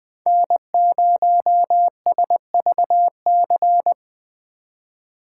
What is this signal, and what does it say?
Morse code, heard as a keyed tone, about 20 wpm, 700 Hz. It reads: N0SVC
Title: Speed Racing - USA 1x3 Call Signs 17wpm